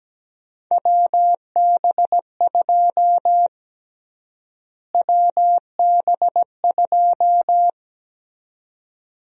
Morse code, heard as a keyed tone, about 17 wpm, 700 Hz. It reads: WB2 WB2
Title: USA Call Sign Prefixes - Repeated 2x - 17wpm